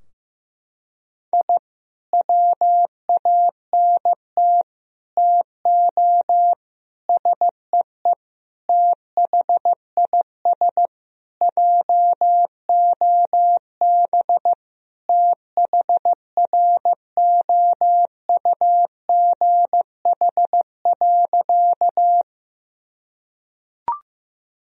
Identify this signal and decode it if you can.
Morse code, heard as a keyed tone, about 16 wpm, 700 Hz. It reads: I WANT TO SEE THIS JOB THROUGH.
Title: Sentences from Top 1000 Words - Review Part 3 - 15wpm